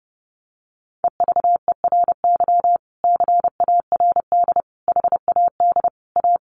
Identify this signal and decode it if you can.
Morse code, heard as a keyed tone, about 30 wpm, 700 Hz. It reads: EVERY CARD HAD A
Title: Sentences from Top 1400 Words 30wpm